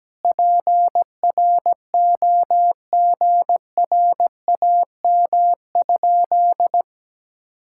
Morse code, heard as a keyed tone, about 17 wpm, 700 Hz. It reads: PROGRAM?
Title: Sentences from Top 1000 Words - Review Part 5 - 17wpm